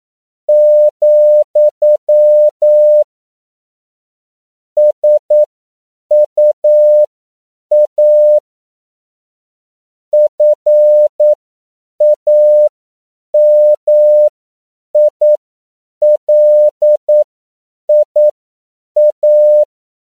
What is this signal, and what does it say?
Morse code, heard as a keyed tone, about 9 wpm, spaced as for 7 wpm, 600 Hz. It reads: , SUA FAMILIA
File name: CW / 10PPM